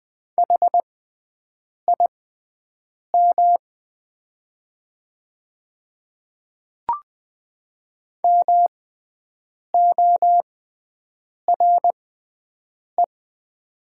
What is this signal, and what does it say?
Morse code, heard as a keyed tone, about 20 wpm, spaced as for 7 wpm, 700 Hz. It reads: HIM MORE
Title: 6x Character Spacing - Top 100 Words - 20wpm